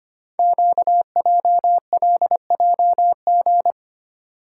Morse code, heard as a keyed tone, about 25 wpm, 700 Hz. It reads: QJLJG
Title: Code Groups - 5 Letters - 25wpm